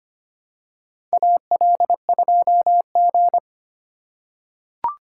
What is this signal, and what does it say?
Morse code, heard as a keyed tone, about 25 wpm, 700 Hz. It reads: AL2G
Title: USA Call Signs - 4 Characters - 3x Character Spacing - 25wpm